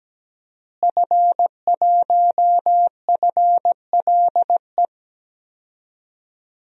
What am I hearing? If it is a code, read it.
Morse code, heard as a keyed tone, about 17 wpm, 700 Hz. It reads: F1FLE
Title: DXCC-100 Call Signs - 5 Characters - Repeated 2x - 17wpm